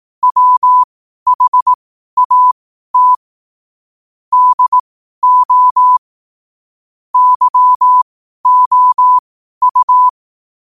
Morse code, heard as a keyed tone, about 18 wpm, spaced as for 12 wpm, 1000 Hz. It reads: WHAT DO YOU
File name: qotdfiles / 12wpm